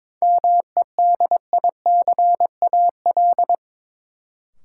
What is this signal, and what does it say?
Morse code, heard as a keyed tone, about 21 wpm, 700 Hz. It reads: MEDICAL
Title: Sentences from Top 800 Words - New Word Drill - 22wpm